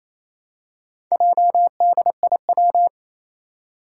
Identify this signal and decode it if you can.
Morse code, heard as a keyed tone, about 27 wpm, 700 Hz. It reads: JDIW
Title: Four Challenging Letter-Number Combinations 28wpm